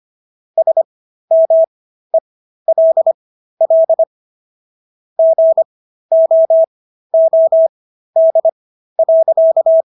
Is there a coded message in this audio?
Yes: Morse code from 0.6 to 9.8 s, heard as a keyed tone, about 25 wpm, spaced as for 13 wpm, 650 Hz. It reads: SMELL GOOD.